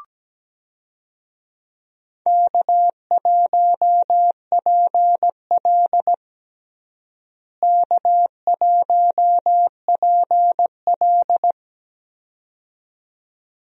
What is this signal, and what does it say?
Morse code, heard as a keyed tone, about 17 wpm, 700 Hz. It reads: K1PL K1PL
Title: USA Call Signs - 4 Characters - Repeated 2x - 17wpm